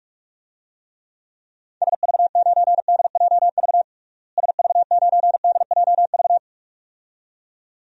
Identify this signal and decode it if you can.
Morse code, heard as a keyed tone, about 46 wpm, 700 Hz. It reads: SV9DJV SV9DJV
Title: DXCC-100 Call Signs - 6 Characters - Repeated 2x - 45wpm